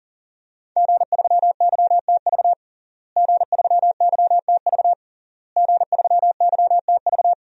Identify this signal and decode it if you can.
Morse code, heard as a keyed tone, about 40 wpm, 700 Hz. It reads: G3YTV G3YTV G3YTV
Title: DXCC-100 Call Signs - 5 Characters - Repeated 3x - 40wpm